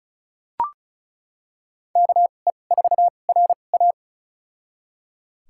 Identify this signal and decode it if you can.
Morse code, heard as a keyed tone, about 36 wpm, spaced as for 25 wpm, 700 Hz. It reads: KE4RA
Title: USA Call Signs - 2x Character Spacing - 35wpm